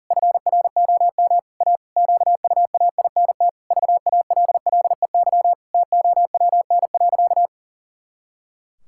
Morse code, heard as a keyed tone, about 37 wpm, 700 Hz. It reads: FROM A QUAINT VALLEY TOWN.
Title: Pangram Challenge v2 - 40wpm